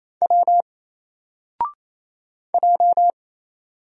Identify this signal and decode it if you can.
Morse code, heard as a keyed tone, about 27 wpm, 700 Hz. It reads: W J
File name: Single-Letter-Number-28wpm